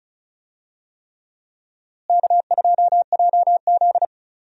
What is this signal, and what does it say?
Morse code, heard as a keyed tone, about 34 wpm, 700 Hz. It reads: K2JZ
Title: USA Call Signs - 4 Characters - Repeated 3x - 35wpm